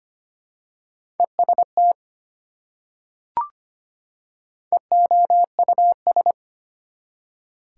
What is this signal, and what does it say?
Morse code, heard as a keyed tone, about 25 wpm, 690 Hz. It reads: EST EOUS
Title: Prefixes and Suffixes 25wpm